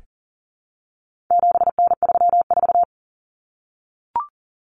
Morse code, heard as a keyed tone, about 40 wpm, 700 Hz. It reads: 7N34
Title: Common Cut Numbers - 4-Digit 40wpm